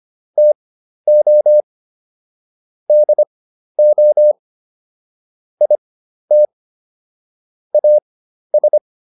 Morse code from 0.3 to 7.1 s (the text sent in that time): TO DO IT